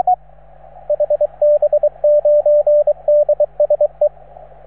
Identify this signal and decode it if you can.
Morse code, heard as a keyed tone, about 25 wpm, 590 Hz, with radio noise. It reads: HB9DSE